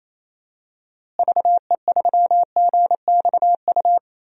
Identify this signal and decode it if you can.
Morse code, heard as a keyed tone, about 28 wpm, 700 Hz. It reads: VE3GXU
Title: DXCC-100 Call Signs - 6 Characters - Repeated 2x - 28wpm